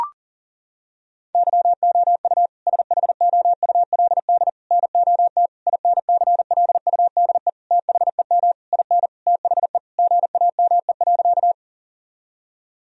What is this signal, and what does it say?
Morse code, heard as a keyed tone, about 40 wpm, 700 Hz. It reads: YOU SHOULD NOT INCLUDE THEM IN THE GAME.
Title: Sentences from Top 1000 Words - Review Part 3 - 40wpm